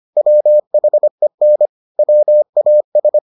Morse code, heard as a keyed tone, about 25 wpm, 600 Hz. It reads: WHEN WAS